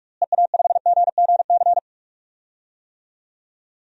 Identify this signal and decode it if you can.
Morse code, heard as a keyed tone, about 46 wpm, 700 Hz. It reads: EA5GGC
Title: DXCC-100 Call Signs - 6 Characters - Repeated 2x - 45wpm